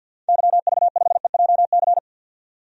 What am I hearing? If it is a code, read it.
Morse code, heard as a keyed tone, about 52 wpm, 700 Hz. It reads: YV5EJC
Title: DXCC-100 Call Signs - 6 Characters - 50wpm